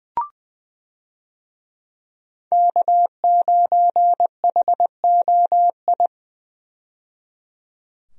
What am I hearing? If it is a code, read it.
Morse code, heard as a keyed tone, about 21 wpm, 700 Hz. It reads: K9HOI
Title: USA Call Signs - 5 Characters - 20wpm